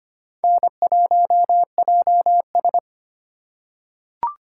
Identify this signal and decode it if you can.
Morse code, heard as a keyed tone, about 25 wpm, 700 Hz. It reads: N1JS
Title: USA Call Signs - 4 Characters - 3x Character Spacing - 25wpm